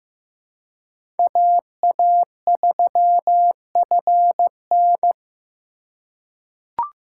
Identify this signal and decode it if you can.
Morse code, heard as a keyed tone, about 15 wpm, 700 Hz. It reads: AA3FN
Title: USA Call Signs - 5 Characters - 3x Character Spacing - 15wpm